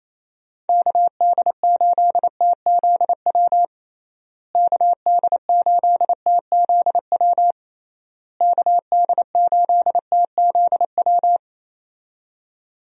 Morse code, heard as a keyed tone, about 28 wpm, 700 Hz. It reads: KD8TZW KD8TZW KD8TZW
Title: USA Call Signs - 6 Characters - Repeated 3x - 28wpm